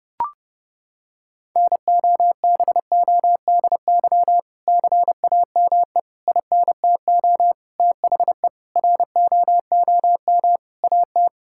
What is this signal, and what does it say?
Morse code, heard as a keyed tone, about 32 wpm, 700 Hz. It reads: NOBODY CAME INTO THE ROOM AT
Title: Sentences from Top 1600 Words 30wpm